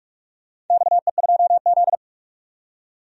Morse code, heard as a keyed tone, about 46 wpm, 700 Hz. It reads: XE2Z
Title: DXCC-100 Call Signs - 4 Characters - 45wpm